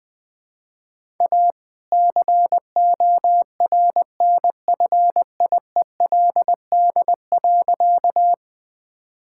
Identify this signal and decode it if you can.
Morse code, heard as a keyed tone, about 20 wpm, 700 Hz. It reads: A CORNFIELD.
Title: All-Age Appropriate Jokes - 20wpm